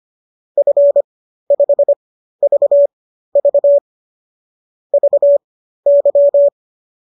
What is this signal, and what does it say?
Morse code, heard as a keyed tone, about 25 wpm, spaced as for 13 wpm, 590 Hz. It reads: F5VV VY
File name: qso214 13